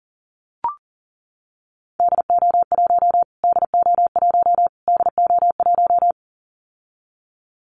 Morse code, heard as a keyed tone, about 37 wpm, 700 Hz. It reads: DO1 DO1 DO1
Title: Callsign Prefixes for Top 25 DXCC - Repeated 3x - 40wpm